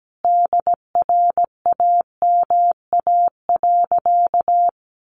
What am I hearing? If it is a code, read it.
Morse code, heard as a keyed tone, about 17 wpm, 700 Hz. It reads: DRAMA.
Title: Sentences from Top 1500 Words 17wpm